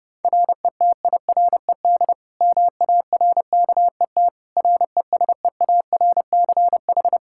Morse code, heard as a keyed tone, about 32 wpm, 700 Hz. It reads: RETIRED MARKET RESEARCH
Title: Practice QSO - Exchange 3 - More Info - 30wpm